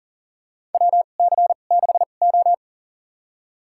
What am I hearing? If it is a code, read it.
Morse code, heard as a keyed tone, about 40 wpm, spaced as for 29 wpm, 700 Hz. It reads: WC6O
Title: USA Call Signs - 4 Characters - 2x Character Spacing - 40wpm